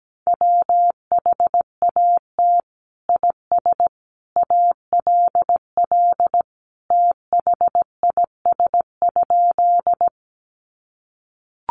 Morse code, heard as a keyed tone, about 17 wpm, 700 Hz. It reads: WHAT IS ALL THIS?
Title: Sentences from Top 100 Words 17wpm